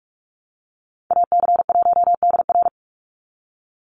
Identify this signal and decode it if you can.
Morse code, heard as a keyed tone, about 44 wpm, 700 Hz. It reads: AC1DR
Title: USA Call Signs 45wpm